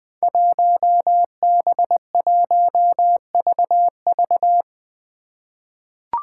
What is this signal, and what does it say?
Morse code, heard as a keyed tone, about 21 wpm, 700 Hz. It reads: 1B1VV
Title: Five Challenging Letter-Number Combinations 20wpm